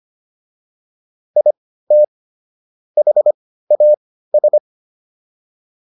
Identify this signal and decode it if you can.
Morse code, heard as a keyed tone, about 25 wpm, spaced as for 15 wpm, 600 Hz. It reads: IT HAS